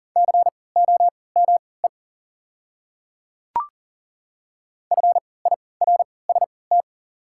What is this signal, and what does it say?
Morse code, heard as a keyed tone, about 40 wpm, spaced as for 23 wpm, 690 Hz. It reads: COME FIRST
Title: 3x Character Spacing - Top 200 Words - 40wpm